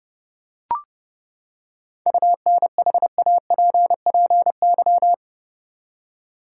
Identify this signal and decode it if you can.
Morse code, heard as a keyed tone, about 30 wpm, 700 Hz. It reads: UNHAPPY
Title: Seven Letter Words - 30wpm